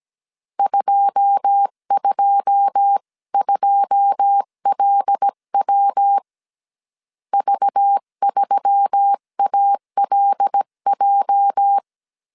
Morse code, heard as a keyed tone, about 17 wpm, 800 Hz. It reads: 222LW V3ALJ
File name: 1500C S B 16wpm